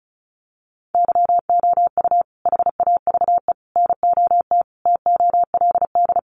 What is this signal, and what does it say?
Morse code, heard as a keyed tone, about 34 wpm, 700 Hz. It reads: YOU HAVE NOT TOLD
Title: Sentences from Top 1000 Words 35wpm